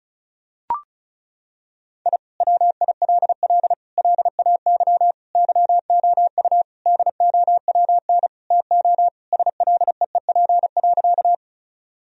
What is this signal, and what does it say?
Morse code, heard as a keyed tone, about 34 wpm, 700 Hz. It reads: I WILL LAY YOU DOWN TO SLEEP.
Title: Sentences from Top 1000 Words - Review Part 2 - 35wpm